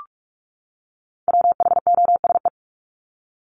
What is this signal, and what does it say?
Morse code, heard as a keyed tone, about 44 wpm, 700 Hz. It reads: WHOSE
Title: Top 1001-1250 Words 45wpm